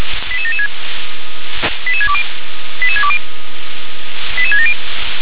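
Five-tone selective call (ZVEI), heard as quick runs of five tones, tones of about 70 ms. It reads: 88776 88620 88620 88670
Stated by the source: Five-tone ZVEI selective call received by SDR.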